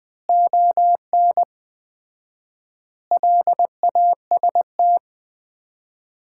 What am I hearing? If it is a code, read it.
Morse code, heard as a keyed tone, about 21 wpm, 700 Hz. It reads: ON LAST
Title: Common QSO Phrases - 4x Word-Spacing - 20wpm